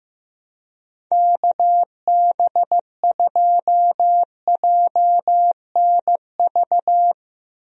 Morse code, heard as a keyed tone, about 15 wpm, 700 Hz. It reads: KB2JNV
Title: USA Call Signs - 6 Characters - Repeated 3x - 15wpm